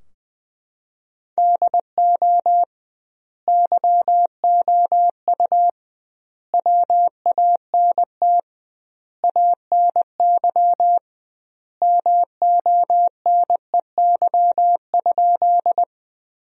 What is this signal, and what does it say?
Morse code, heard as a keyed tone, about 21 wpm, 700 Hz. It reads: DO YOU WANT ANY MONEY?
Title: 2x Word-Spacing - Sentences from Top 400 Words - 20wpm